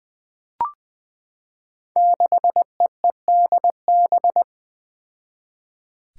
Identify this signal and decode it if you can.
Morse code, heard as a keyed tone, about 20 wpm, 700 Hz. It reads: 6EEDB
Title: Five Challenging Letter-Number Combinations 20wpm